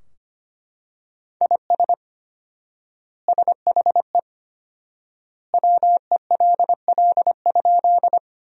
Morse code, heard as a keyed tone, about 25 wpm, 700 Hz. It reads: IS SHE WELL?
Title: 4x Word-Spacing - Sentences from Top 200 Words - 25wpm